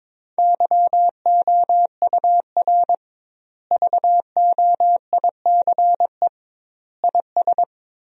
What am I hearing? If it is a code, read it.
Morse code, heard as a keyed tone, about 22 wpm, 700 Hz. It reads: YOUR VOICE IS
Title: 2x Word-Spacing - Sentences from Top 500 Words - 22wpm